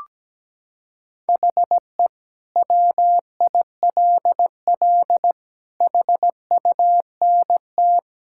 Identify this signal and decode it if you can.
Morse code, heard as a keyed tone, about 17 wpm, 700 Hz. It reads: HE WILL HUNT
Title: Sentences from Top 1600 Words 17wpm